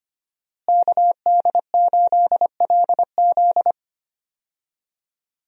Morse code, heard as a keyed tone, about 25 wpm, 700 Hz. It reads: KD8LZ
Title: USA Call Signs - 5 Characters - Repeated 3x - 25wpm